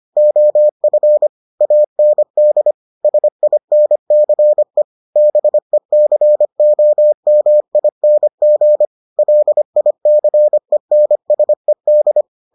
Morse code, heard as a keyed tone, about 25 wpm, 600 Hz. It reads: OF AND SINCE BECOMING LICENSED